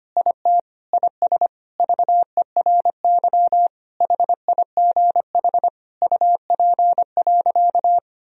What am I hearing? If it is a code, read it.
Morse code, heard as a keyed tone, about 25 wpm, 700 Hz. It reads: IT IS VERY HIGH UP.
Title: Sentences from Top 1000 Words - Review Part 1 - 25wpm